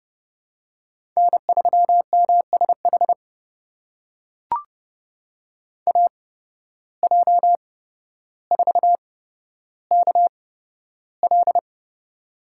Morse code, heard as a keyed tone, about 30 wpm, spaced as for 13 wpm, 700 Hz. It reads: N3MSH AJ4KL
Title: USA Call Signs - 5 Characters - 8x Character Spacing - 30wpm